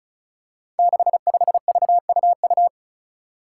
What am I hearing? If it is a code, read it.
Morse code, heard as a keyed tone, about 34 wpm, 700 Hz. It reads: 65VUU
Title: Five Challenging Letter-Number Combinations 35wpm